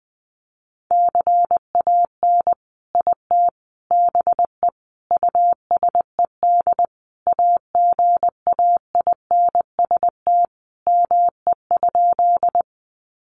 Morse code, heard as a keyed tone, about 20 wpm, 700 Hz. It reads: CAN IT BE USED AGAINST ME?
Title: Sentences from Top 600 Words 20wpm